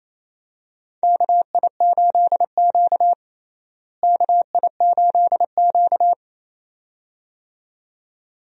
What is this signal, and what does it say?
Morse code, heard as a keyed tone, about 28 wpm, 700 Hz. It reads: KI8Q KI8Q
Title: USA Call Signs - 4 Characters - Repeated 2x - 28wpm